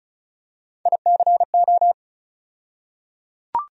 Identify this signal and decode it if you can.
Morse code, heard as a keyed tone, about 36 wpm, 700 Hz. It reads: ICO
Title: USA Call Sign Postfixes - 3 Letters 35wpm